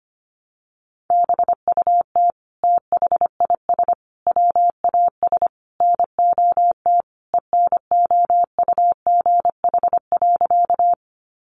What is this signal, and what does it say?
Morse code, heard as a keyed tone, about 25 wpm, 700 Hz. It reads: BUT THIS WAS NOT ENOUGH.
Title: Sentences from Top 600 Words 25wpm